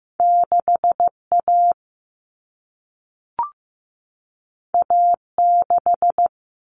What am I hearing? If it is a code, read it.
Morse code, heard as a keyed tone, about 15 wpm, 700 Hz. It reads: 6A A6
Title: All Cut Numbers - 2-Digit 15wpm